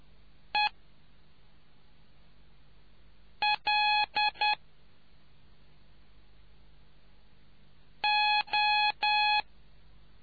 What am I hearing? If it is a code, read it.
Morse code, heard as a keyed tone, about 10 wpm, 840 Hz. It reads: E L O